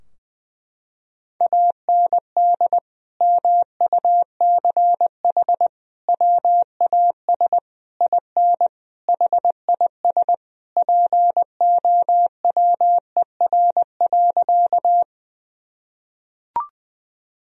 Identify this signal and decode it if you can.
Morse code, heard as a keyed tone, about 19 wpm, 700 Hz. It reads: AND MUCH WAS IN HIS POWER.
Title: Sentences from Top 1000 Words - Review Part 2 - 20wpm